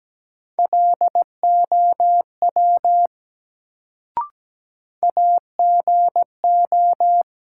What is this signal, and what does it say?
Morse code, heard as a keyed tone, about 17 wpm, 700 Hz. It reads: LOW AGO
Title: Sentences from Top 500 Words - New Word Drill - 17wpm